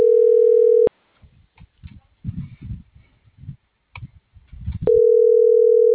Ringback tone (North America), 440 + 480 Hz.
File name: CENTRAL MAERSK / out adolfo.msk